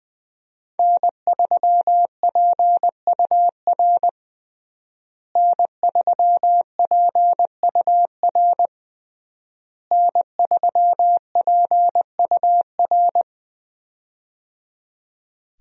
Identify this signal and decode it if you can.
Morse code, heard as a keyed tone, about 20 wpm, 700 Hz. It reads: N3PUR N3PUR N3PUR
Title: USA Call Signs - 5 Characters - Repeated 3x - 20wpm